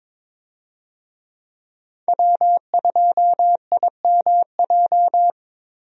Morse code, heard as a keyed tone, about 21 wpm, 700 Hz. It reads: W2IMJ
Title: USA Call Signs - 5 Characters - Repeated 3x - 22wpm